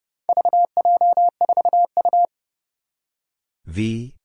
Morse code, heard as a keyed tone, about 30 wpm, 700 Hz. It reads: VJ4U
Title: Four Challenging Letter-Number Combinations 30wpm